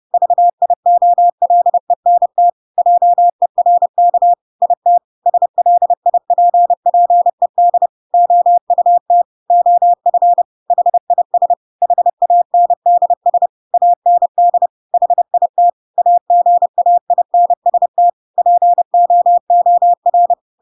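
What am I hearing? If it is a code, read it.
Morse code, heard as a keyed tone, about 30 wpm, 700 Hz. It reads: VIOLENT JERK IT SLIPPED OUT OF HIS HANDS AND HIT AGAINST POOR
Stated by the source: Chapter 2 of The Adventures of Pinocchio read as a book in Morse code (CW) at 30 wpm, by ik1hge.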